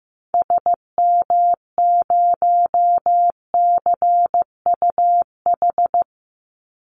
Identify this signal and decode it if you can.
Morse code, heard as a keyed tone, about 15 wpm, 700 Hz. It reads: SM0CUH
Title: Call Signs Repeated 2x - 15wpm